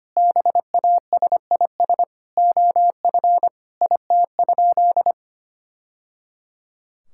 Morse code, heard as a keyed tone, about 25 wpm, 700 Hz. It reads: BASIS OF IT?
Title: Sentences from Top 1000 Words - Review Part 5 - 25wpm